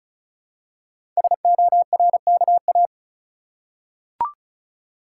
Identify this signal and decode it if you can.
Morse code, heard as a keyed tone, about 36 wpm, 700 Hz. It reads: SORKA
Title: Code Groups - 5 Letters - 35wpm